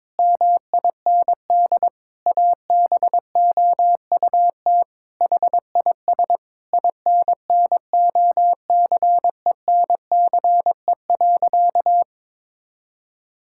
Morse code, heard as a keyed tone, about 21 wpm, 700 Hz. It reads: MIND ABOUT HIS INNOCENCE.
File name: Sentences with 7-Word Phrases - 22wpm